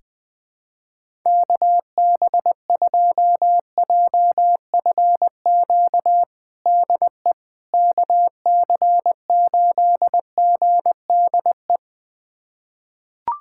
Morse code, heard as a keyed tone, about 20 wpm, 700 Hz. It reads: KB2JFQ DE KC8GDE
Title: USA Callsign DE Callsign - 6 Characters - 20wpm